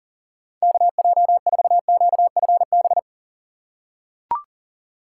Morse code, heard as a keyed tone, about 40 wpm, 700 Hz. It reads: KJ4QFB
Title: USA Call Signs - 2x Character Spacing - 40wpm